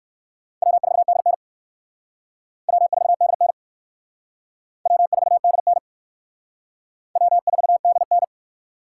Morse code, heard as a keyed tone, about 58 wpm, 700 Hz. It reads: W4DN W4DN W4DN W4DN